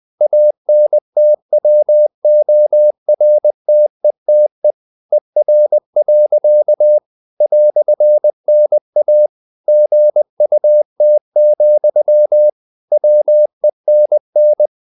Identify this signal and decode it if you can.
Morse code, heard as a keyed tone, about 20 wpm, 600 Hz. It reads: ANTWORTETE ER. "NA GUT, WENN
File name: dk5ke / Humor-100BpM